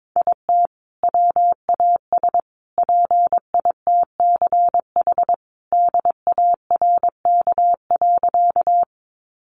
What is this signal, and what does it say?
Morse code, heard as a keyed tone, about 22 wpm, 700 Hz. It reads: IT WAS PITCH DARK.
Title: Sentences from Top 1300 Words 22wpm